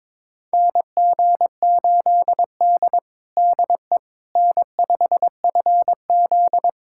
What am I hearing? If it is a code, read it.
Morse code, heard as a keyed tone, about 22 wpm, 700 Hz. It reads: NG8D DE N5FZ
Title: USA Callsign DE Callsign - 4 Characters - 22wpm